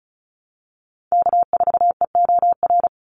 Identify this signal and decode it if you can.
Morse code, heard as a keyed tone, about 34 wpm, 700 Hz. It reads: K4EOR
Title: Call Signs - 35wpm